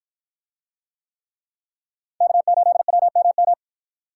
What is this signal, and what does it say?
Morse code, heard as a keyed tone, about 53 wpm, 700 Hz. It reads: K8WMM